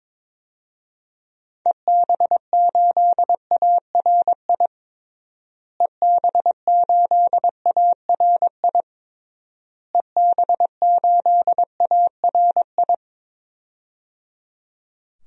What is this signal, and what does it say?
Morse code, heard as a keyed tone, about 21 wpm, 700 Hz. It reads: EB8ARI EB8ARI EB8ARI
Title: DXCC-100 Call Signs - 6 Characters - Repeated 3x - 22wpm